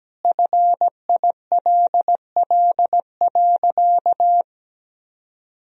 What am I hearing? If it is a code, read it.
Morse code, heard as a keyed tone, about 17 wpm, 700 Hz. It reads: FILL.
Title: Sentences from Top 1000 Words - Review Part 3 - 17wpm